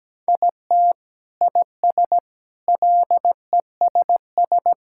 Morse code, heard as a keyed tone, about 17 wpm, 700 Hz. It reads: IT IS LESS
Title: Sentences from Top 1000 Words - Review Part 6 - 17wpm